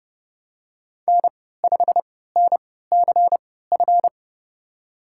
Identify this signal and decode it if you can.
Morse code, heard as a keyed tone, about 30 wpm, spaced as for 17 wpm, 700 Hz. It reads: N5NCF
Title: USA Call Signs - 5 Characters - 3x Character Spacing - 30wpm